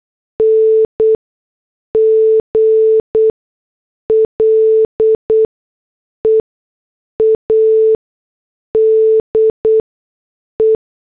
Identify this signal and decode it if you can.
Morse code, heard as a keyed tone, about 8 wpm, 440 Hz. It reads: N G L E A D E
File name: morse / headlines